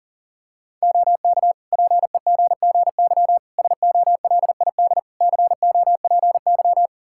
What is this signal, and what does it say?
Morse code, heard as a keyed tone, about 40 wpm, 700 Hz. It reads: OK PEGGY SOLID COPY
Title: Practice QSO - Exchange 3 - More Info - 40wpm